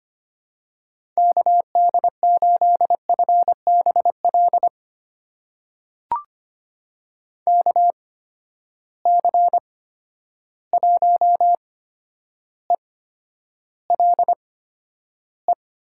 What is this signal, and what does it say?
Morse code, heard as a keyed tone, about 25 wpm, spaced as for 11 wpm, 700 Hz. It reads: KD8FBL KC1ELE
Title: USA Call Signs - 6 Characters - 8x Character Spacing - 25wpm